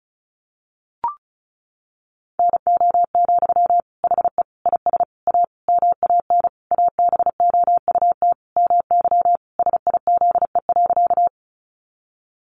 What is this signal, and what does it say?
Morse code, heard as a keyed tone, about 34 wpm, 700 Hz. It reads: NO, HE IS A MAN ABOUT MY SIZE.
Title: Sentences from Top 400 Words 35wpm